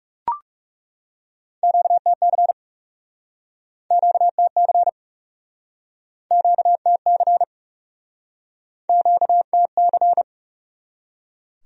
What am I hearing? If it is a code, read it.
Morse code, heard as a keyed tone, about 40 wpm, 700 Hz. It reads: QTC QTC QTC QTC